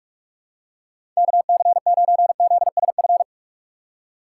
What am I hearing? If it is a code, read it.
Morse code, heard as a keyed tone, about 44 wpm, 700 Hz. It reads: KC9ZSF
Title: USA Call Signs - 6x Character Spacing - 45wpm